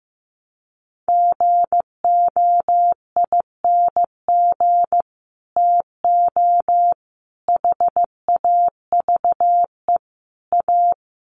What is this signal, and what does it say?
Morse code, heard as a keyed tone, about 15 wpm, 700 Hz. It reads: GOING TO HAVE A
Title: Sets of 4 Words 15wpm